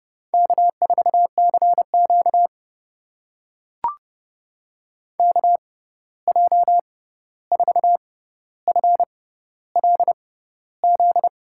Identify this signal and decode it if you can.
Morse code, heard as a keyed tone, about 32 wpm, spaced as for 14 wpm, 700 Hz. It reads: K4CQ KJ4FLZ
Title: USA Call Signs - 6x Character Spacing - 30wpm